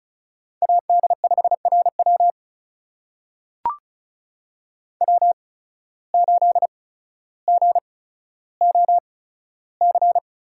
Morse code, heard as a keyed tone, about 34 wpm, spaced as for 15 wpm, 700 Hz. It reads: AD5RW W8GOC
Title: USA Call Signs - 5 Characters - 8x Character Spacing - 35wpm